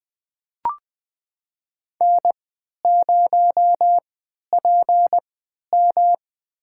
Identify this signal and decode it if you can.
Morse code, heard as a keyed tone, about 20 wpm, spaced as for 11 wpm, 700 Hz. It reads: N0PM
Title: USA Call Signs - 4 Characters - 3x Character Spacing - 20wpm